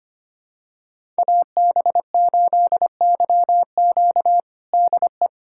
Morse code, heard as a keyed tone, about 25 wpm, 700 Hz. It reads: AB8YQ DE
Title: USA Callsign DE Callsign 25wpm